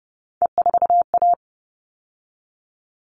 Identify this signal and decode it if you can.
Morse code, heard as a keyed tone, about 30 wpm, 700 Hz. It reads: E4A
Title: All Cut Numbers - 3-Digit 30wpm